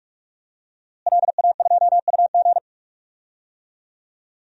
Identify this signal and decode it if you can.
Morse code, heard as a keyed tone, about 44 wpm, 700 Hz. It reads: LA2UG
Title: DXCC-100 Call Signs - 5 Characters - Repeated 2x - 45wpm